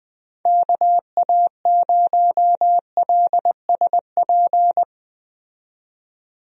Morse code, heard as a keyed tone, about 20 wpm, 700 Hz. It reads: KA0LSP
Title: USA Call Signs - 6 Characters - 20wpm